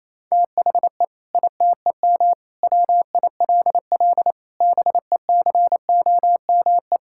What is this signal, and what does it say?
Morse code, heard as a keyed tone, about 28 wpm, 700 Hz. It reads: THE ITEM WILL BECOME
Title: Sentences from Top 1000 Words - Review Part 6 - 28wpm